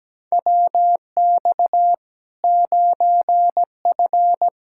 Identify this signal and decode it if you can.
Morse code, heard as a keyed tone, about 17 wpm, 700 Hz. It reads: WX 9F
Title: K1USN SST QSO Practice 17wpm